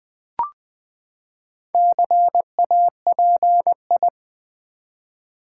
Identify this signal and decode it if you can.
Morse code, heard as a keyed tone, about 20 wpm, 700 Hz. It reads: CAPI
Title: Build A Word - 4 Letters At A Time - 10-Letter Words - 20wpm